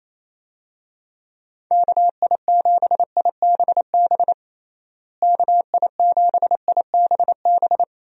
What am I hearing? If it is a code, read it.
Morse code, heard as a keyed tone, about 28 wpm, 700 Hz. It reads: KI7IBB KI7IBB
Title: USA Call Signs - 6 Characters - Repeated 2x - 28wpm